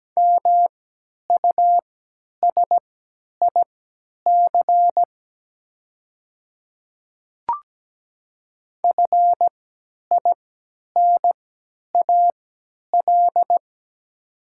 Morse code, heard as a keyed tone, about 17 wpm, spaced as for 10 wpm, 700 Hz. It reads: MUSIC FINAL
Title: 3x Character Spacing - Top 500 Words - 17wpm